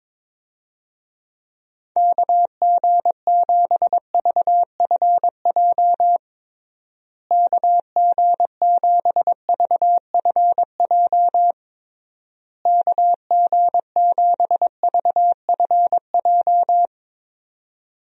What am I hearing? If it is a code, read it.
Morse code, heard as a keyed tone, about 22 wpm, 700 Hz. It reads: KG7VFJ KG7VFJ KG7VFJ
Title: USA Call Signs - 6 Characters - Repeated 3x - 22wpm